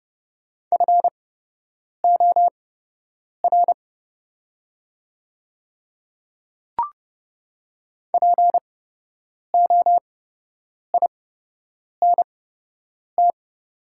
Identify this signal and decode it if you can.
Morse code, heard as a keyed tone, about 30 wpm, spaced as for 8 wpm, 700 Hz. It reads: FOR POINT
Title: 8x Character Spacing - Top 200 Words - 30wpm